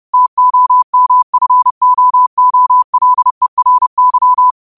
Morse code, heard as a keyed tone, about 30 wpm, 1000 Hz. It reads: TOMFOOLERY
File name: qotdfiles / 30wpm